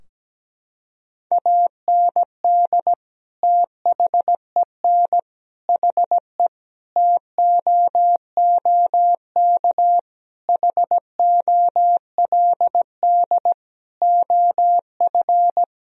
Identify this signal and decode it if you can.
Morse code, heard as a keyed tone, about 17 wpm, 700 Hz. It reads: AND THEN HE TOOK HOLD OF
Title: Sentences from Top 1000 Words - Review Part 2 - 17wpm